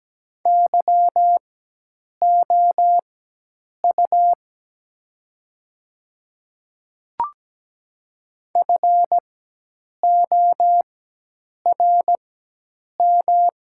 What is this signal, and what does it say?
Morse code, heard as a keyed tone, about 17 wpm, spaced as for 8 wpm, 700 Hz. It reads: YOU FORM
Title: 4x Character Spacing - Top 500 Words - 17wpm